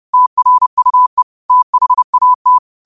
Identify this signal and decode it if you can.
Morse code, heard as a keyed tone, about 30 wpm, 1000 Hz. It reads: TRUE THAT
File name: qotdfiles / 30wpm